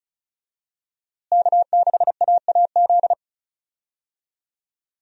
Morse code, heard as a keyed tone, about 36 wpm, 700 Hz. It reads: K6AAZ
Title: USA Call Signs - 5 Characters - Repeated 3x - 35wpm